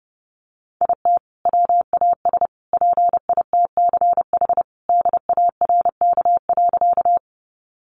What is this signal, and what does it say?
Morse code, heard as a keyed tone, about 30 wpm, 700 Hz. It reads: IT WAS PITCH DARK.
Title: Sentences from Top 1300 Words 30wpm